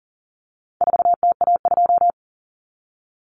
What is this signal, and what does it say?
Morse code, heard as a keyed tone, about 40 wpm, 700 Hz. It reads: 4TA2
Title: Common Cut Numbers - 4-Digit 40wpm